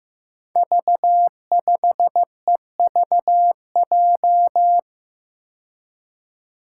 Morse code, heard as a keyed tone, about 15 wpm, 700 Hz. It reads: V5EVJ
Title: Five Challenging Letter-Number Combinations 15wpm